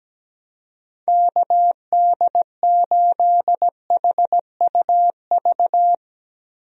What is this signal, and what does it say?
Morse code, heard as a keyed tone, about 17 wpm, 700 Hz. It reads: KD8HUV
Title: USA Call Signs - 6 Characters - Repeated 2x - 17wpm